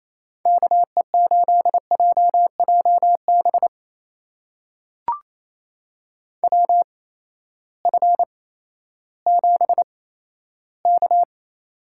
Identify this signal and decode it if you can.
Morse code, heard as a keyed tone, about 28 wpm, spaced as for 14 wpm, 700 Hz. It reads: KE8JJB WF7K
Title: USA Call Signs - 8x Character Spacing - 28wpm